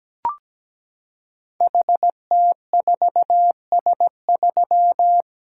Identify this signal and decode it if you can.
Morse code, heard as a keyed tone, about 17 wpm, 700 Hz. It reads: HT4S3
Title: Code Groups - 5 Letter:Numbers - 17wpm